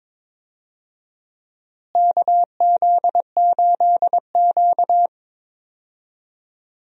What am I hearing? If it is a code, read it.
Morse code, heard as a keyed tone, about 22 wpm, 700 Hz. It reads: KZ8Q
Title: USA Call Signs - 4 Characters - Repeated 2x - 22wpm